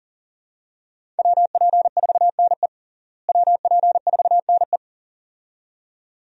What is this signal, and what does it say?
Morse code, heard as a keyed tone, about 40 wpm, 700 Hz. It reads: WP4NE WP4NE
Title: USA Call Signs - 5 Characters - Repeated 2x - 40wpm